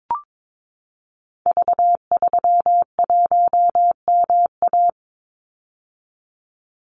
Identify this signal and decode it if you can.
Morse code, heard as a keyed tone, about 22 wpm, 700 Hz. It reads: V31MA
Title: Call Signs - 22wpm